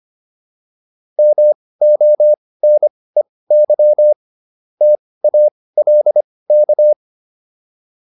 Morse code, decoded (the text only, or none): MONEY TALK